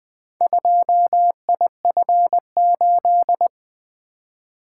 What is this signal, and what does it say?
Morse code, heard as a keyed tone, about 20 wpm, 700 Hz. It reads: 2IF8
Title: Four Letter-Number Combinations - 20wpm